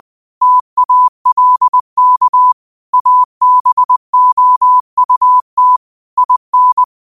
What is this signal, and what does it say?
Morse code, heard as a keyed tone, about 20 wpm, 1000 Hz. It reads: TALK ABOUT IN